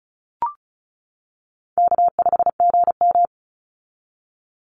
Morse code, heard as a keyed tone, about 36 wpm, 700 Hz. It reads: K5GM
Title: Call Signs - 35wpm